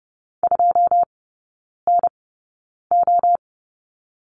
Morse code, heard as a keyed tone, about 30 wpm, spaced as for 9 wpm, 700 Hz. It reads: 2NO
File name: Lesson 16e - Letter c - Three Characters 30wpm